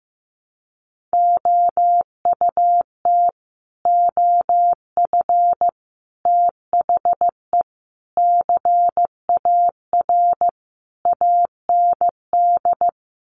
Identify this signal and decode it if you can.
Morse code, heard as a keyed tone, about 15 wpm, 700 Hz. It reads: OUT OF THE CAR AND
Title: Sets of 5 Words Encore 15wpm